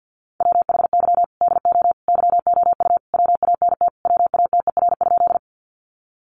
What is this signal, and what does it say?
Morse code, heard as a keyed tone, about 50 wpm, 700 Hz. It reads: WHY DO YOU WANT WATER?
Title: Sentences from Top 200 Words 50wpm